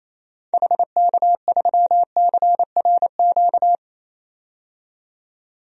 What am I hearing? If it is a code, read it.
Morse code, heard as a keyed tone, about 28 wpm, 700 Hz. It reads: HK3CRQ
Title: DXCC-100 Call Signs - 6 Characters - 28wpm